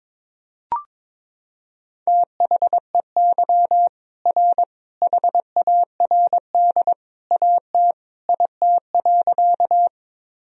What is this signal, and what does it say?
Morse code, heard as a keyed tone, about 22 wpm, 700 Hz. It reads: THEY R HARD AT IT.
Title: Sentences from Top 300 Words and Abbreviations 22wpm